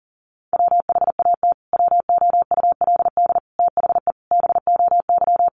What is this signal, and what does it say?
Morse code, heard as a keyed tone, about 40 wpm, 700 Hz. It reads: WHAT WOULD THE BOY
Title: Sentences from Top 1300 Words 40wpm